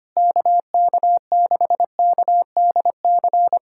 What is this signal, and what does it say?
Morse code, heard as a keyed tone, about 25 wpm, 700 Hz. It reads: KK6KDC
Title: USA Callsign DE Callsign - 6 Characters - 25wpm